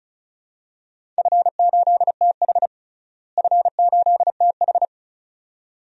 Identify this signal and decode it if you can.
Morse code, heard as a keyed tone, about 36 wpm, 700 Hz. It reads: F8TH F8TH
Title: DXCC-100 Call Signs - 4 Characters - Repeated 2x - 35wpm